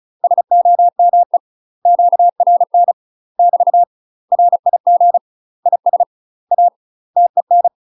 Morse code, heard as a keyed tone, about 34 wpm, 700 Hz. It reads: SOME QRN <BT> RIG IS A TEN
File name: UK MASTER QSOs TRAINING / 35WPM / UK MASTER QSOs 0003